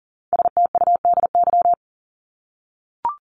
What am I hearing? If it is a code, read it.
Morse code, heard as a keyed tone, about 40 wpm, 700 Hz. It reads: STUDY
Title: Top 300 Words 40wpm